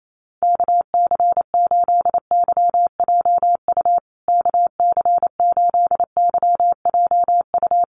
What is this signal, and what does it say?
Morse code, heard as a keyed tone, about 28 wpm, 700 Hz. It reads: KC8YJU KC8YJU
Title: USA Call Signs Repeated 3x 28wpm